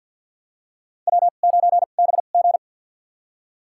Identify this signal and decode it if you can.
Morse code, heard as a keyed tone, about 52 wpm, spaced as for 37 wpm, 700 Hz. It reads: W9BG
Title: USA Call Signs - 4 Characters - 2x Character Spacing - 50wpm